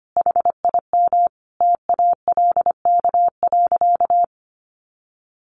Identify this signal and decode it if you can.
Morse code, heard as a keyed tone, about 25 wpm, 700 Hz. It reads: HIM TALK.
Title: Sentences from Top 600 Words 25wpm